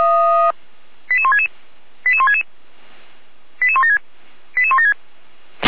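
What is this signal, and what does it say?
Five-tone selective call (ZVEI), heard as quick runs of five tones, tones of about 80 ms. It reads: 88166 88166 88176 88176
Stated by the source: ZVEI five-tone selective call received on an SDR.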